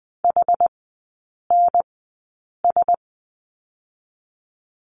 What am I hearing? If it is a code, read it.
Morse code, heard as a keyed tone, about 20 wpm, spaced as for 8 wpm, 700 Hz. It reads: HNS
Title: 060 Lesson 11e - Letter h - Three Characters 20wpm